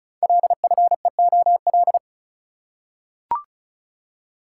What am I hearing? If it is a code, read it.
Morse code, heard as a keyed tone, about 36 wpm, 690 Hz. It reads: LFEOL
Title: Code Groups - 5 Letters - 35wpm